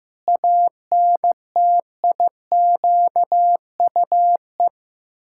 Morse code, heard as a keyed tone, about 15 wpm, 700 Hz. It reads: ANTIQUE
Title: Pangram Challenge v2 - 15wpm